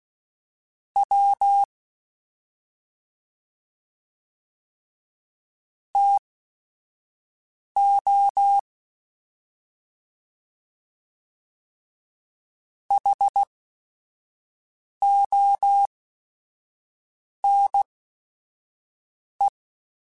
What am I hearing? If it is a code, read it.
Morse code, heard as a keyed tone, about 16 wpm, spaced as for 5 wpm, 800 Hz. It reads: W TO HONE
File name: T then H Runs / T then H Runs.mp3s / T then H Runs @5